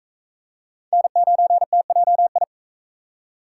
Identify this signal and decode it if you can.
Morse code, heard as a keyed tone, about 40 wpm, 700 Hz. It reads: N9TJI